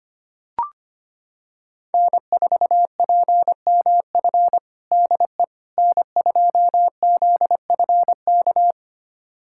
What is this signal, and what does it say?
Morse code, heard as a keyed tone, about 25 wpm, 700 Hz. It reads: N4PMF DE N2ZFK
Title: USA Callsign DE Callsign - 5 Characters - 25wpm